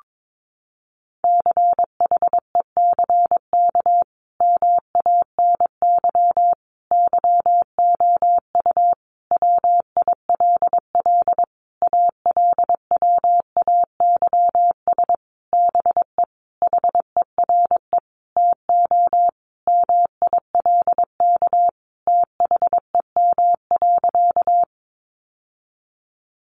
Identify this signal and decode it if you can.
Morse code, heard as a keyed tone, about 22 wpm, 700 Hz. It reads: CHECK MANY YOU WILL ALWAYS BE HERE TO MILK THEM.
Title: Sentences from Top 1500 Words 22wpm